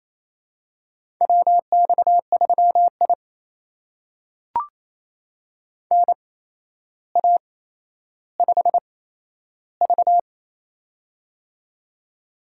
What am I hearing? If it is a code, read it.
Morse code, heard as a keyed tone, about 27 wpm, spaced as for 12 wpm, 700 Hz. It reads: WX3I NA5V
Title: USA Call Signs - 4 Characters - 8x Character Spacing - 28wpm